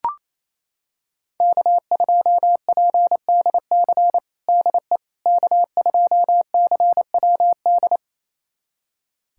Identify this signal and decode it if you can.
Morse code, heard as a keyed tone, about 28 wpm, 700 Hz. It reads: K2PDC DE K2CWD
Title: USA Callsign DE Callsign - 5 Characters - 28wpm